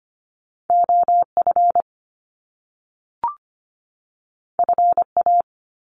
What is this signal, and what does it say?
Morse code, heard as a keyed tone, about 25 wpm, 700 Hz. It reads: OF FA
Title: Top Two Letter Combinations in Words 25wpm